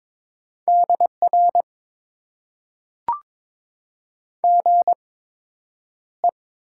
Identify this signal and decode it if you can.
Morse code, heard as a keyed tone, about 22 wpm, spaced as for 9 wpm, 700 Hz. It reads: DR GE
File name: Top 90 QSO Elements - 8x Character Spacing - 22wpm